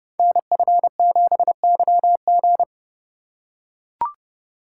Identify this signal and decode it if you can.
Morse code, heard as a keyed tone, about 30 wpm, 700 Hz. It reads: NF7YG
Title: Code Groups - 5 Letter:Numbers - 30wpm